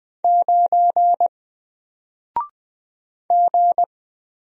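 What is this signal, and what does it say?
Morse code, heard as a keyed tone, about 20 wpm, 700 Hz. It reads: 9 G
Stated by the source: Single letter and number Morse drill at 20 wpm, from N4LCD.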